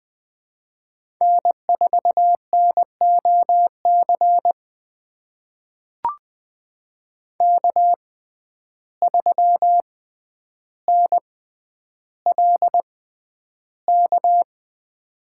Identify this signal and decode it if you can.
Morse code, heard as a keyed tone, about 20 wpm, spaced as for 10 wpm, 700 Hz. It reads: N4NOC K3NLK
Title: USA Call Signs - 5 Characters - 6x Character Spacing - 20wpm